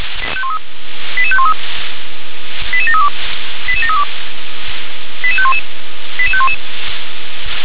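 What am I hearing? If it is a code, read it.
Five-tone selective call (ZVEI), heard as quick runs of five tones, tones of about 70 ms. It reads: 88513 88732 88732 88511 88511
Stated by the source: ZVEI five-tone call 88743 received by SDR.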